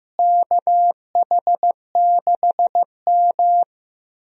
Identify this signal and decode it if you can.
Morse code, heard as a keyed tone, about 16 wpm, 700 Hz. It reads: KH6M
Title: DXCC-100 Call Signs - 4 Characters - Repeated 2x - 15wpm